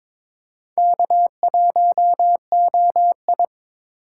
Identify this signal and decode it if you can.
Morse code, heard as a keyed tone, about 22 wpm, 700 Hz. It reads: K1OI
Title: USA Call Signs - 8x Character Spacing - 22wpm